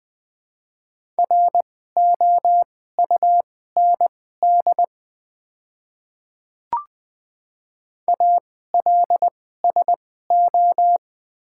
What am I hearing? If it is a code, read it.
Morse code, heard as a keyed tone, about 20 wpm, spaced as for 14 wpm, 700 Hz. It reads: ROUND ALSO
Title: 2x Character Spacing - Top 500 Words - 20wpm